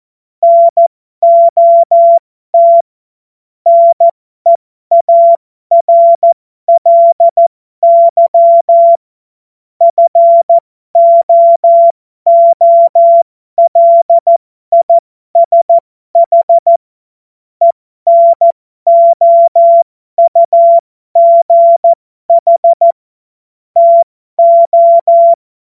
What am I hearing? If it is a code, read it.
Morse code, heard as a keyed tone, about 14 wpm, spaced as for 12 wpm, 680 Hz. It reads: NOT NEARLY FOOLISH ENOUGH TO
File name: Morsekurs / Witze